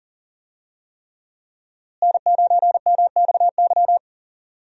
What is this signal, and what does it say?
Morse code, heard as a keyed tone, about 40 wpm, 700 Hz. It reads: N9MXY
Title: USA Call Signs - 5 Characters - Repeated 3x - 40wpm